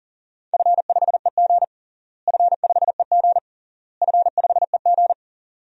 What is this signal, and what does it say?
Morse code, heard as a keyed tone, about 40 wpm, 700 Hz. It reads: F5EG F5EG F5EG
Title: DXCC-100 Call Signs - 4 Characters - Repeated 3x - 40wpm